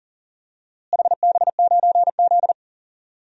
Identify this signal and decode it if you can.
Morse code, heard as a keyed tone, about 38 wpm, 700 Hz. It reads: HB9Z
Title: DXCC-100 Call Signs - 4 Characters - Repeated 3x - 40wpm